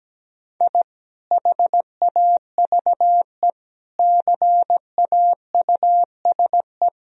Morse code, heard as a keyed tone, about 17 wpm, 700 Hz. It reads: I HAVE CAUSE
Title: Sentences from Top 1000 Words - Review Part 1 - 17wpm